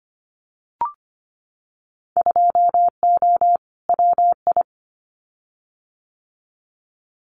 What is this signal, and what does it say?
Morse code, heard as a keyed tone, about 25 wpm, 700 Hz. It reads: 2O WI
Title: Winter Field Day Exchanges 25wpm